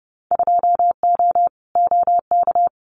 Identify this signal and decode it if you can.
Morse code, heard as a keyed tone, about 30 wpm, 700 Hz. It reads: 2O OK
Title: Winter Field Day Exchanges Encore 30wpm